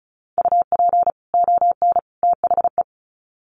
Morse code, heard as a keyed tone, about 34 wpm, 700 Hz. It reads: UP ON THE
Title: Sets of 3 Words 35wpm